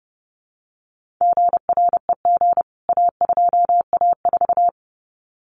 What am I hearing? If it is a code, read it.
Morse code, heard as a keyed tone, about 30 wpm, 700 Hz. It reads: GREG A2A4
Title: CWT Exchange 30wpm